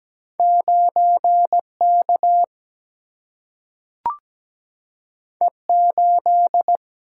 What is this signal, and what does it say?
Morse code, heard as a keyed tone, about 17 wpm, 700 Hz. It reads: 9K E8
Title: Single-Letter Single-Number 17wpm